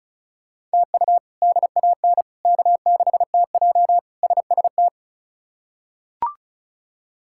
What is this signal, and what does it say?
Morse code, heard as a keyed tone, about 36 wpm, 700 Hz. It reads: TU DAN K6TJ SST
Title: K1USN SST QSO Practice - 35wpm